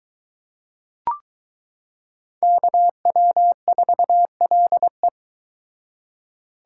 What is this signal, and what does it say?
Morse code, heard as a keyed tone, about 22 wpm, 700 Hz. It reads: KW4LE